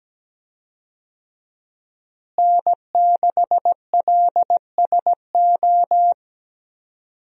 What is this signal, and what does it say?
Morse code, heard as a keyed tone, about 17 wpm, 700 Hz. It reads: N6LSO
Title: USA Call Signs - 5 Characters - Repeated 2x - 17wpm